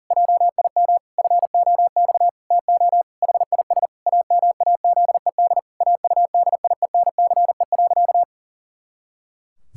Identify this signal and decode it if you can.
Morse code, heard as a keyed tone, about 40 wpm, 700 Hz. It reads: JIM FOX TO HIS AMAZED AUDIENCE.
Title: Pangram Challenge v2 - 40wpm